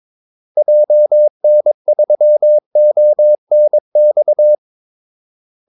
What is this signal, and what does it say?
Morse code, heard as a keyed tone, about 23 wpm, 600 Hz. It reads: JN3ONX
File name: Callsigns-22wpm